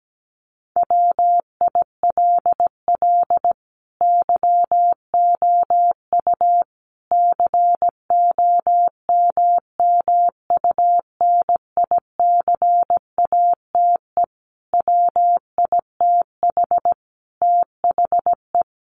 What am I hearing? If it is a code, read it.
Morse code, heard as a keyed tone, about 17 wpm, 700 Hz. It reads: WILL YOU COMMUNICATE WITH THE
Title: Sentences from Top 1300 Words 17wpm